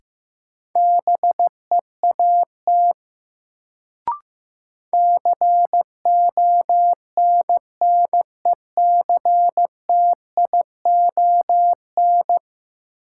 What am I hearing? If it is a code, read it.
Morse code, heard as a keyed tone, about 15 wpm, 700 Hz. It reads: BEAT CONNECTION
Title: Sentences from Top 1300 Words - New Word Drill Alt - 15wpm